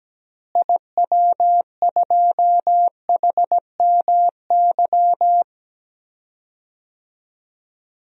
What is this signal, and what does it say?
Morse code, heard as a keyed tone, about 17 wpm, 700 Hz. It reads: IW2HMY
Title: DXCC-100 Call Signs - 6 Characters - Repeated 2x - 17wpm